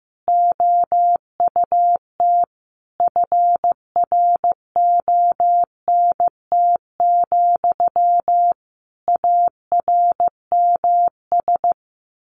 Morse code, heard as a keyed tone, about 14 wpm, 700 Hz. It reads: OUT FRONT, ARMS
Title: A Sound Of Thunder 15wpm